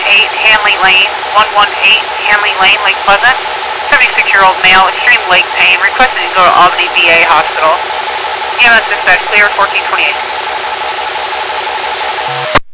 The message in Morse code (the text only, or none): none